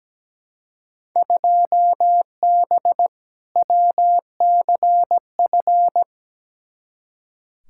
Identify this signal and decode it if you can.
Morse code, heard as a keyed tone, about 17 wpm, 700 Hz. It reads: 2B WCF
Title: Field Day Exchanges Encore - 17wpm